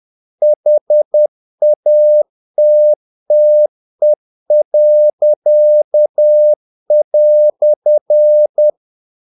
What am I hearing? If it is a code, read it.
Morse code, heard as a keyed tone, about 10 wpm, 600 Hz. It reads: HATTE."
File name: dk5ke / Humor-50BpM